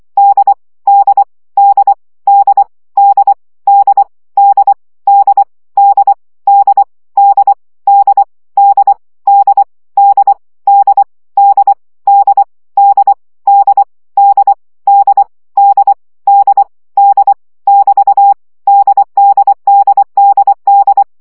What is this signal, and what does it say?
Morse code, heard as a keyed tone, about 24 wpm, 800 Hz. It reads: D D D D D D D D D D D D D D D D D D D D D D D D D <BT> DDDDD
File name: CW gyakorlas d 20 baud